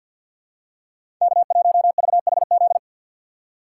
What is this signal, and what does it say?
Morse code, heard as a keyed tone, about 52 wpm, 700 Hz. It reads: K1VHZ
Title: USA Call Signs - 5 Characters - 4x Character Spacing - 50wpm